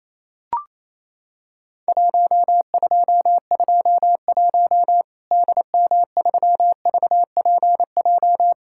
Morse code, heard as a keyed tone, about 28 wpm, 700 Hz. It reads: 1221 DM3VPJ
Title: Worked All Europe DX Contest - QTC Exchange - 28wpm